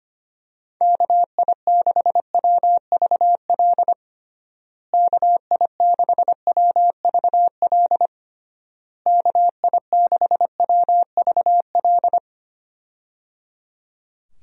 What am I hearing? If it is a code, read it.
Morse code, heard as a keyed tone, about 25 wpm, 700 Hz. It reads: KI6WVL KI6WVL KI6WVL
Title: USA Call Signs - 6 Characters - Repeated 3x - 25wpm